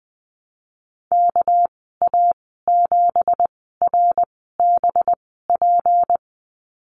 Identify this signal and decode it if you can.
Morse code, heard as a keyed tone, about 20 wpm, spaced as for 15 wpm, 700 Hz. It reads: KA7RBP
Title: 231 Lesson 37g - Letter x - Call Signs 20wpm